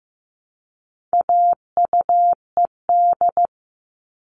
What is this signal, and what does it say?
Morse code, heard as a keyed tone, about 15 wpm, 700 Hz. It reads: AUED
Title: All Cut Numbers - 4-Digit 15wpm